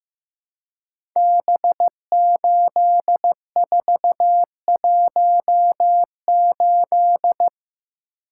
Morse code, heard as a keyed tone, about 15 wpm, 700 Hz. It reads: B8418
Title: Code Groups - 5 Letter:Numbers - 15wpm